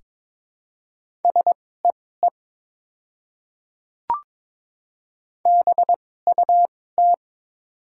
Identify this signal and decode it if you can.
Morse code, heard as a keyed tone, about 22 wpm, spaced as for 16 wpm, 700 Hz. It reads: SEE BUT
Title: 2x Character Spacing - Top 100 Words - 22wpm